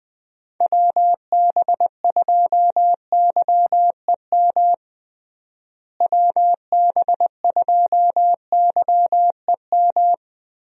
Morse code, heard as a keyed tone, about 20 wpm, 700 Hz. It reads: WB2YEM WB2YEM
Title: USA Call Signs - 6 Characters - Repeated 3x - 20wpm